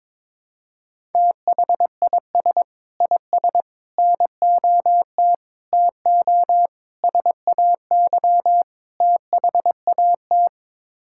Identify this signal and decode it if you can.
Morse code, heard as a keyed tone, about 22 wpm, 700 Hz. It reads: THIS IS NOT TO SAY THAT
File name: practice / 22wpm / Sets of 6 Words - 22wpm